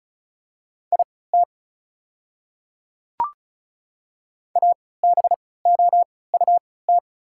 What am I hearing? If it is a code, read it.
Morse code, heard as a keyed tone, about 34 wpm, spaced as for 20 wpm, 690 Hz. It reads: IT ABOUT
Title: 3x Character Spacing - Top 100 Words - 35wpm